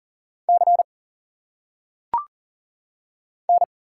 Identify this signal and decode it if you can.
Morse code, heard as a keyed tone, about 40 wpm, 700 Hz. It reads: C N
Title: Single Letters - Phonetic Pronunciation 40wpm